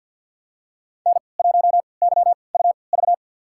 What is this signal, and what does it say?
Morse code, heard as a keyed tone, about 49 wpm, spaced as for 28 wpm, 700 Hz. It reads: N1YUV
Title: USA Call Signs - 3x Character Spacing - 50wpm